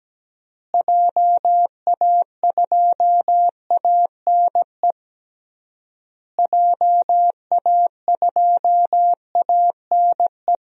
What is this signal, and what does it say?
Morse code, heard as a keyed tone, about 17 wpm, 700 Hz. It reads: JA2ANE JA2ANE
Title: DXCC-100 Call Signs - 6 Characters - Repeated 3x - 17wpm